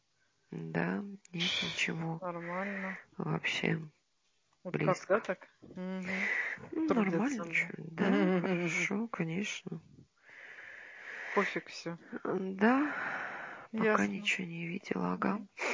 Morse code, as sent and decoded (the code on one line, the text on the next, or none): none